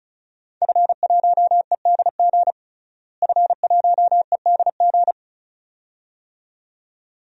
Morse code, heard as a keyed tone, about 36 wpm, 700 Hz. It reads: F1EDG F1EDG
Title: DXCC-100 Call Signs - 5 Characters - Repeated 2x - 35wpm